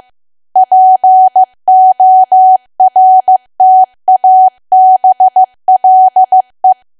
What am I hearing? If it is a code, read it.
Morse code, heard as a keyed tone, about 15 wpm, 750 Hz. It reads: PORTABLE
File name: cwa / W1AW-Level2 / 15WPM